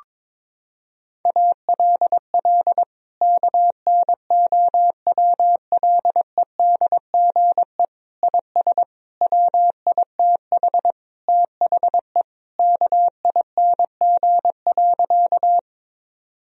Morse code, heard as a keyed tone, about 22 wpm, 700 Hz. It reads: ALL KNOWLEDGE IS WITH THE KING.
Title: Sentences from Top 1000 Words - Review Part 3 - 22wpm